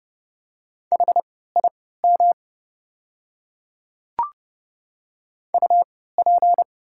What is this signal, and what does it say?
Morse code, heard as a keyed tone, about 31 wpm, spaced as for 17 wpm, 700 Hz. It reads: HIM UP
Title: 3x Character Spacing - Top 200 Words - 30wpm